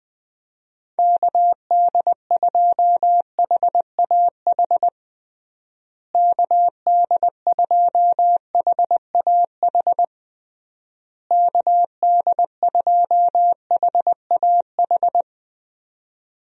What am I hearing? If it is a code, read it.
Morse code, heard as a keyed tone, about 20 wpm, 700 Hz. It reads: KD2HAH KD2HAH KD2HAH
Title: USA Call Signs - 6 Characters - Repeated 3x - 20wpm